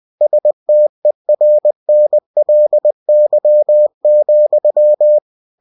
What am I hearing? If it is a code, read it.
Morse code, heard as a keyed tone, about 20 wpm, 600 Hz. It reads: STERNLY,
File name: cw-books / winnie-the-pooh / winnie-20wpm-page-0108